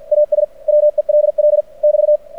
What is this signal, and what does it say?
Morse code, heard as a keyed tone, about 53 wpm, 600 Hz. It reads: AS WELL <BT>
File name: archived / CRSnet